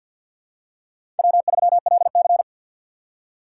Morse code, heard as a keyed tone, about 53 wpm, 700 Hz. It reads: W3LC
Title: USA Call Signs - 4x Character Spacing - 50wpm